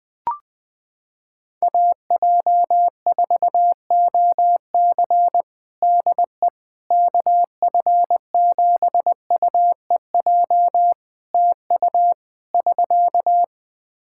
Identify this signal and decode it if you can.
Morse code, heard as a keyed tone, about 20 wpm, 700 Hz. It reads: AJ4OC DE KF7UEJ TU <SK>
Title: Practice QSO - Exchange 4 - Ending - 20wpm